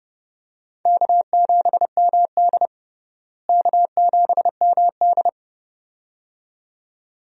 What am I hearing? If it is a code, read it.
Morse code, heard as a keyed tone, about 30 wpm, 700 Hz. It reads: K7MD K7MD
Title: USA Call Signs - 4 Characters - Repeated 2x - 30wpm